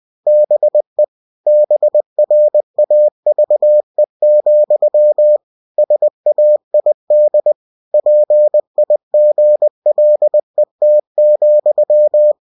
Morse code, heard as a keyed tone, about 20 wpm, 600 Hz. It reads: BE BRAVE, SAID PIGLET,
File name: cw-books / winnie-the-pooh / winnie-20wpm-page-0062